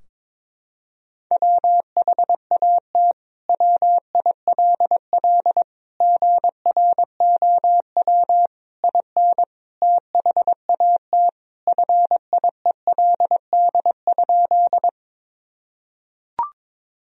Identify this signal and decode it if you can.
Morse code, heard as a keyed tone, about 22 wpm, 700 Hz. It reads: WHAT WILL GROW IN THAT FIELD?
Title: Sentences from Top 1000 Words - Review Part 2 - 22wpm